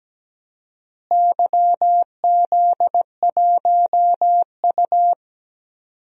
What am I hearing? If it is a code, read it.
Morse code, heard as a keyed tone, about 17 wpm, 700 Hz. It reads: YZ1U
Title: DXCC-100 Call Signs - 4 Characters - Repeated 2x - 17wpm